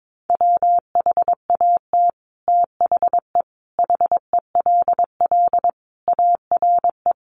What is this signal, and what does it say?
Morse code, heard as a keyed tone, about 22 wpm, 700 Hz. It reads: WHAT THE HELL ARE
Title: Sets of 5 Words Encore 22wpm